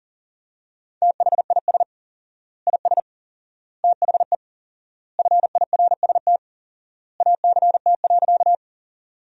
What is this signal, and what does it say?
Morse code, heard as a keyed tone, about 40 wpm, 700 Hz. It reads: THIS IS THE FIRST ACT.
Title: 4x Word-Spacing - Sentences from Top 200 Words - 40wpm